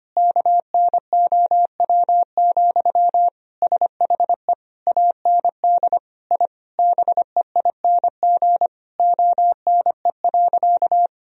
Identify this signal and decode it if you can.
Morse code, heard as a keyed tone, about 25 wpm, 700 Hz. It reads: KNOW, SHE AND I BEING ONE.
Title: Sentences from Top 1000 Words - Review Part 3 - 25wpm